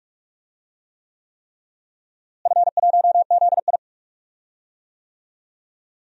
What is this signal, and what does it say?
Morse code, heard as a keyed tone, about 47 wpm, 700 Hz. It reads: F1ZI
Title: DXCC-100 Call Signs - 4 Characters - Repeated 3x - 45wpm